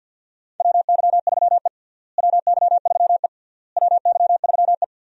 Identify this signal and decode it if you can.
Morse code, heard as a keyed tone, about 50 wpm, 700 Hz. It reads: WY3E WY3E WY3E
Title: USA Call Signs - 4 Characters - Repeated 3x - 50wpm